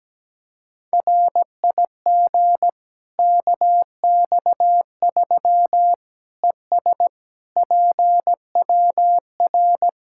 Morse code, heard as a keyed tone, about 17 wpm, 700 Hz. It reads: RIG KX3 ES PWR
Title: Practice QSO - Exchange 2 - Info - 17wpm